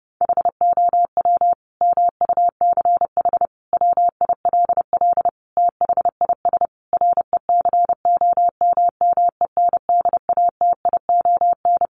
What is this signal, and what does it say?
Morse code, heard as a keyed tone, about 30 wpm, 700 Hz. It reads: HOW MUCH WILL THIS RECOMMENDATION
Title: Sentences from Top 1400 Words 30wpm